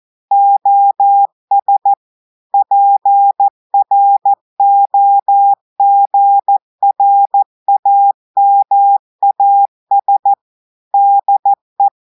Morse code, heard as a keyed tone, about 14 wpm, 800 Hz. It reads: OS PROGRAMAS DE
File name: audios / portugais / portugais texte en clair 2 vitesse 14 mots minutes